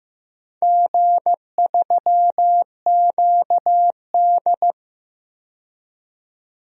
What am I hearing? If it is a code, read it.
Morse code, heard as a keyed tone, about 15 wpm, 700 Hz. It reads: G3QD
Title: DXCC-100 Call Signs - 4 Characters - 15wpm